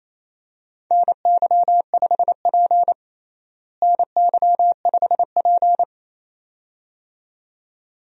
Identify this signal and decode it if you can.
Morse code, heard as a keyed tone, about 28 wpm, 700 Hz. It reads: NY5P NY5P
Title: USA Call Signs - 4 Characters - Repeated 2x - 28wpm